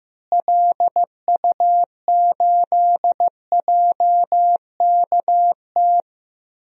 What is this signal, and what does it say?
Morse code, heard as a keyed tone, about 15 wpm, 700 Hz. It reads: LU8JKT
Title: DXCC-100 Call Signs - 6 Characters - Repeated 3x - 15wpm